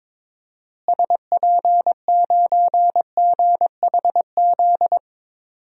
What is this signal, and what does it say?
Morse code, heard as a keyed tone, about 22 wpm, 700 Hz. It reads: SP9GHZ
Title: DXCC-100 Call Signs - 6 Characters - Repeated 2x - 22wpm